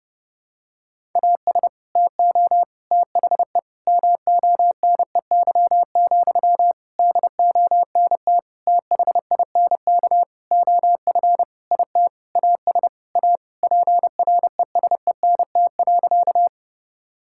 Morse code, heard as a keyed tone, about 32 wpm, 700 Hz. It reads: AS TO THE MONEY, DONT THINK OF IT AS A PRESENT.
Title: Sentences from Top 1000 Words - Review Part 5 - 30wpm